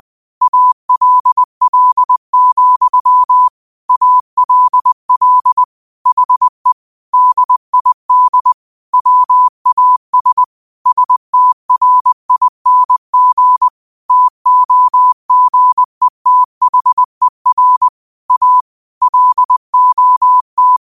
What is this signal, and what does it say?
Morse code, heard as a keyed tone, about 20 wpm, 1000 Hz. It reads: ALL, ALL HE DID WAS STRING TOGETHER A LOT
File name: qotdfiles / 20wpm